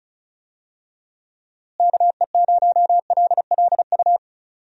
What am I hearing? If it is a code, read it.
Morse code, heard as a keyed tone, about 36 wpm, 700 Hz. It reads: KE0LLU
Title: USA Call Signs - 6 Characters - Repeated 3x - 35wpm